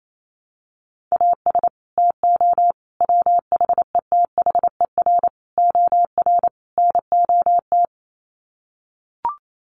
Morse code, heard as a keyed tone, about 28 wpm, 700 Hz. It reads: AS TO WHETHER OR NOT
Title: Sets of 5 Words Encore 28wpm